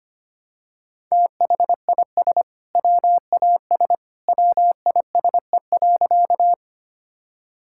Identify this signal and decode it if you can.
Morse code, heard as a keyed tone, about 25 wpm, 700 Hz. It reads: THIS WAS WISE.
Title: Sentences from Top 1600 Words 25wpm